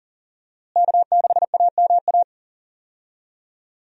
Morse code, heard as a keyed tone, about 40 wpm, 700 Hz. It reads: K6AMA
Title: USA Call Signs - 5 Characters - Repeated 2x - 40wpm